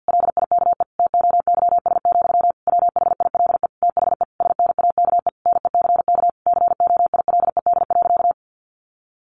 Morse code, heard as a keyed tone, about 50 wpm, 700 Hz. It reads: LIKE TOYS, WHILE THE SNAKE NECK COILED.
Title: A Sound Of Thunder 50wpm